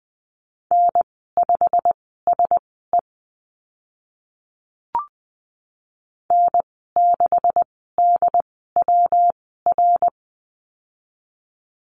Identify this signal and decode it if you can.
Morse code, heard as a keyed tone, about 20 wpm, spaced as for 15 wpm, 700 Hz. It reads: N5SE N6DWR
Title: 207 Lesson 33g - Letter j - Call Signs 20wpm